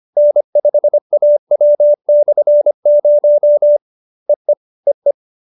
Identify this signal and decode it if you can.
Morse code, heard as a keyed tone, about 25 wpm, 590 Hz. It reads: N5AW/0 EE EE